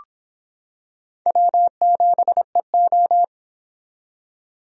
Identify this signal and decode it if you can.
Morse code, heard as a keyed tone, about 24 wpm, 700 Hz. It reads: W7EO